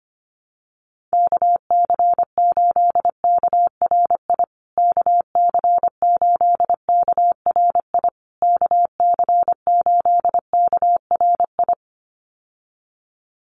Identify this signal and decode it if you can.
Morse code, heard as a keyed tone, about 25 wpm, 700 Hz. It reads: KC8KRI KC8KRI KC8KRI
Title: USA Call Signs Repeated 3x 25wpm